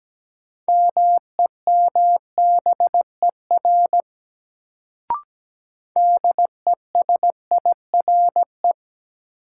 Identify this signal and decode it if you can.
Morse code, heard as a keyed tone, about 17 wpm, 700 Hz. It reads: MEMBER DESIRE
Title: Sentences from Top 1100 Words - New Word Drill - 17wpm